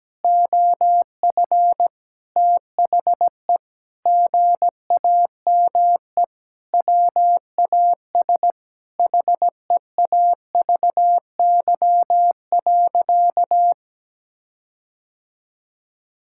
Morse code, heard as a keyed tone, about 17 wpm, 700 Hz. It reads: OF THE GAME WAS HEAVY.
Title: Sentences from Top 1600 Words 17wpm